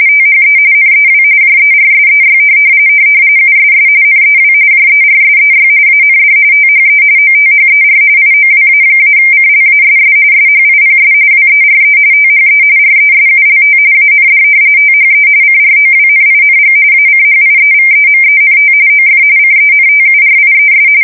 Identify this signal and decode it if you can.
RTTY, about 45.45 baud. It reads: THE NEW JERSEY USS NEW JERSEY AND THE OTHER IOWA-CLASS SHIPS WERE THE RESULT AND THE CROWNING ACHIEVEMENT OF CENTURIES
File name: fsk170 / battleshipnj